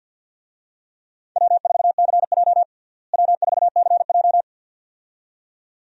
Morse code, heard as a keyed tone, about 50 wpm, 700 Hz. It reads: W4CJ W4CJ
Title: USA Call Signs - 4 Characters - Repeated 2x - 50wpm